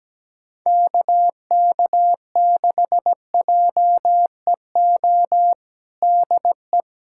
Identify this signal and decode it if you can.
Morse code, heard as a keyed tone, about 17 wpm, 700 Hz. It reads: KK6JEO DE
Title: USA Callsign DE Callsign 17wpm